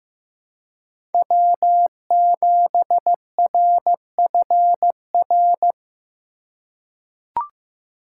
Morse code, heard as a keyed tone, about 16 wpm, 700 Hz. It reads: W7RFR
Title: USA Call Signs - 5 Characters - 2x Character Spacing - 15wpm